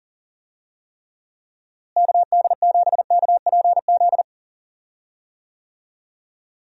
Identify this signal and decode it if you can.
Morse code, heard as a keyed tone, about 40 wpm, 700 Hz. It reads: KD7KPZ
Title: USA Call Signs - 6 Characters - Repeated 2x - 40wpm